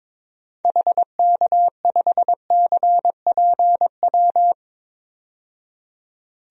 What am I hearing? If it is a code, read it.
Morse code, heard as a keyed tone, about 22 wpm, 700 Hz. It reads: HK5CPW
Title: DXCC-100 Call Signs - 6 Characters - 22wpm